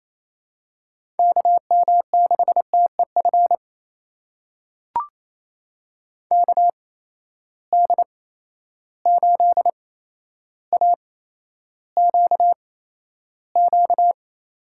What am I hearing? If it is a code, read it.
Morse code, heard as a keyed tone, about 28 wpm, spaced as for 12 wpm, 700 Hz. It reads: KM6TEF KD8AQQ
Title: USA Call Signs - 6 Characters - 8x Character Spacing - 28wpm